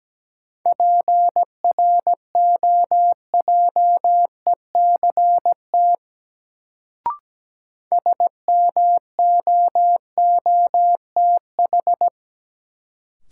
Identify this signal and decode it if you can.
Morse code, heard as a keyed tone, about 17 wpm, 700 Hz. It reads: PROJECT SMOOTH
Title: Sentences from Top 1400 Words - New Word Drill - 17wpm